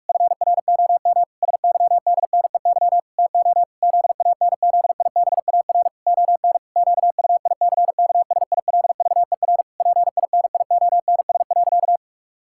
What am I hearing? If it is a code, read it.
Morse code, heard as a keyed tone, about 41 wpm, 700 Hz. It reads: FROM SYDNEY TO ZANZIBAR ON QUICKSILVER PINIONS.
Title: Pangram Challenge v2 - 45wpm